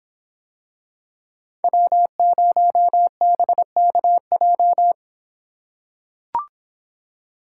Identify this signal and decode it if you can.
Morse code, heard as a keyed tone, about 26 wpm, 700 Hz. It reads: W0BKJ